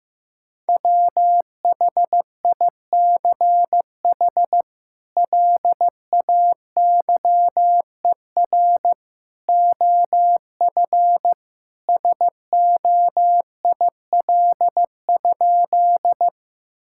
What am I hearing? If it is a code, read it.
Morse code, heard as a keyed tone, about 15 wpm, 700 Hz. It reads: WHICH LAYER OF SOIL?
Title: Sentences from Top 1600 Words 15wpm